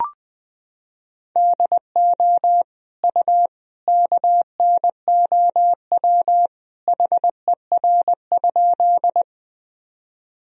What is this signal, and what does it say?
Morse code, heard as a keyed tone, about 20 wpm, 700 Hz. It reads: DO U KNOW HER?
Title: Sentences from Top 100 Words and Abbreviations 20wpm